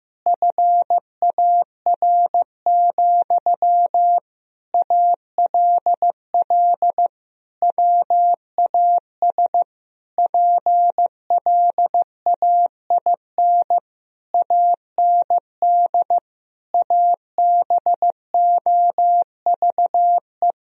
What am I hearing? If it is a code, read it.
Morse code, heard as a keyed tone, about 16 wpm, 700 Hz. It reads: FAR, ALL WAS PLAIN AND ABOVE
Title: Sentences from Top 1000 Words - Review Part 4 - 15wpm